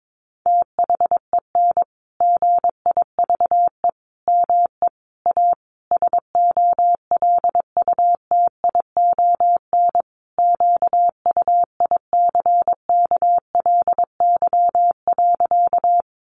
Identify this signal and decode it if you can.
Morse code, heard as a keyed tone, about 22 wpm, 700 Hz. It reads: THEN GIVE ME A SOLUTION QUICKLY.
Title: Sentences from Top 1300 Words 22wpm